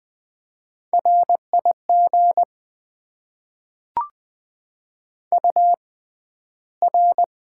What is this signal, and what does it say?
Morse code, heard as a keyed tone, about 20 wpm, spaced as for 12 wpm, 700 Hz. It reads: RIG UR
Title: Top 90 QSO Elements - 6x Character Spacing - 20wpm